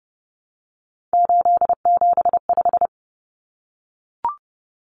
Morse code, heard as a keyed tone, about 30 wpm, 700 Hz. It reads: 875
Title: Numbers - 3-Digit 30wpm